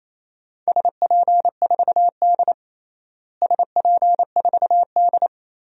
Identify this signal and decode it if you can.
Morse code, heard as a keyed tone, about 28 wpm, 700 Hz. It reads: SP4D SP4D
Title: DXCC-100 Call Signs - 4 Characters - Repeated 3x - 28wpm